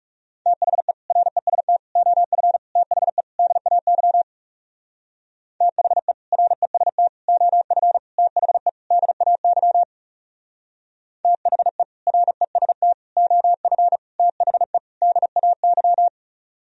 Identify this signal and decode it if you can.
Morse code, heard as a keyed tone, about 36 wpm, 700 Hz. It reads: THE REST OF THE DAY THE REST OF THE DAY THE REST OF THE DAY
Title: Speed-Racing - Sets of 5 Words 30wpm